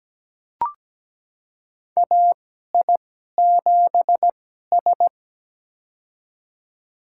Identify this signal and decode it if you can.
Morse code, heard as a keyed tone, about 17 wpm, spaced as for 12 wpm, 700 Hz. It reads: AI7S
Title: USA Call Signs - 4 Characters - 2x Character Spacing - 17wpm